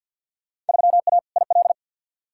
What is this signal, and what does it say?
Morse code, heard as a keyed tone, about 52 wpm, 700 Hz. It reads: 3A IL
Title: Field Day Exchanges - 50wpm